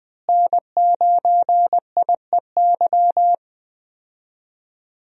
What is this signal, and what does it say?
Morse code, heard as a keyed tone, about 20 wpm, 700 Hz. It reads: N9IEY
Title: USA Call Signs - 5 Characters - Repeated 2x - 20wpm